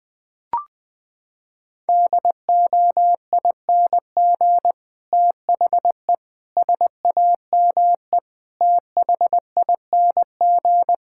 Morse code, heard as a keyed tone, about 20 wpm, 700 Hz. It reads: DOING THE SAME THING
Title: Sets of 8 Words - 20wpm